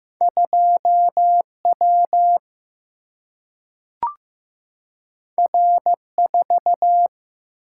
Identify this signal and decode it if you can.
Morse code, heard as a keyed tone, about 15 wpm, 700 Hz. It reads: 2 W R 4
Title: Single-Letter Single-Number 15wpm